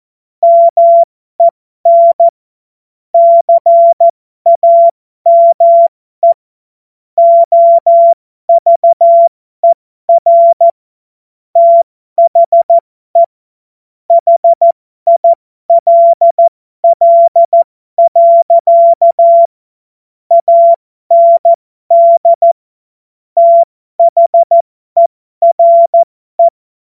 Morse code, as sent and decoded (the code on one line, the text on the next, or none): -- . -. / -.-. .- -- . / --- ...- . .-. / - .... / .... .. .-.. .-.. .-.-.- / .- -. -.. / - .... .-. .
MEN CAME OVER THE HILL. AND THERE